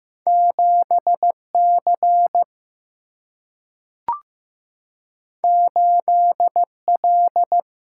Morse code, heard as a keyed tone, about 15 wpm, 700 Hz. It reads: 7C 8L
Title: Single-Letter Single-Number 15wpm